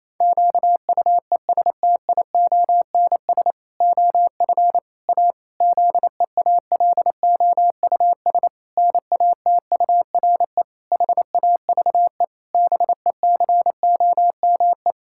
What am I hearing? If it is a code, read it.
Morse code, heard as a keyed tone, about 28 wpm, 700 Hz. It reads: QUESTIONS OF A ZEALOUS NATURE HAVE BECOME
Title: Pangram Challenge v2 - 28wpm